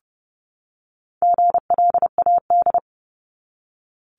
Morse code, heard as a keyed tone, about 30 wpm, 700 Hz. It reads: GLAD
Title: Top 751-1000 Words 30wpm